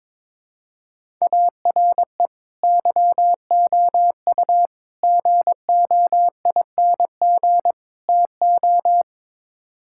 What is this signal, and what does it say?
Morse code, heard as a keyed tone, about 22 wpm, 700 Hz. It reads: ARE YOU GOING TO
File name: practice / 22wpm / Build A Phrase - 5-Word Phrases - 22wpm